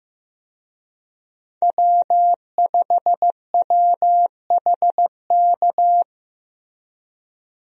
Morse code, heard as a keyed tone, about 15 wpm, 700 Hz. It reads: W5WHK
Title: USA Call Signs - 5 Characters - Repeated 3x - 15wpm